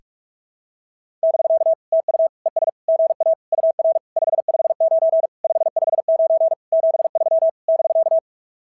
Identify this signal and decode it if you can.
Morse code, heard as a keyed tone, about 44 wpm, 690 Hz. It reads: <BK> TU ES GA UR 559 559 73 <BK>
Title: SOTA QSO Practice 45wpm